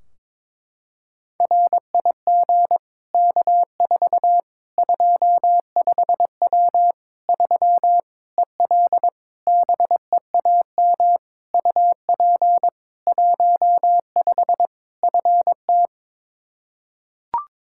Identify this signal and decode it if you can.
Morse code, heard as a keyed tone, about 22 wpm, 700 Hz. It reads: RIG K4 25W 3 EL BEAM UP 15 FT
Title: Practice QSO - Exchange 2 - Info - 22wpm